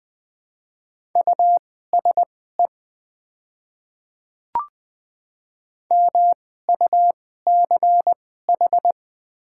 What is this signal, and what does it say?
Morse code, heard as a keyed tone, about 20 wpm, spaced as for 15 wpm, 700 Hz. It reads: USE MUCH
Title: 2x Character Spacing - Top 300 Words - 20wpm